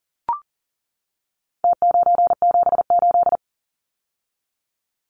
Morse code, heard as a keyed tone, about 40 wpm, 700 Hz. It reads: T978
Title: Numbers - Contest Style - 4-Digit 40wpm